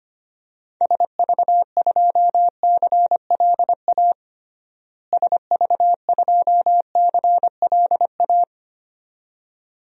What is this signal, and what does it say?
Morse code, heard as a keyed tone, about 25 wpm, 700 Hz. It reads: SV2CLA SV2CLA
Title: DXCC-100 Call Signs - 6 Characters - Repeated 2x - 25wpm